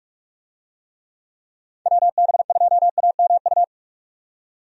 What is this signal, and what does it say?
Morse code, heard as a keyed tone, about 43 wpm, 700 Hz. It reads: WB2AMU
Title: USA Call Signs - 6 Characters - 45wpm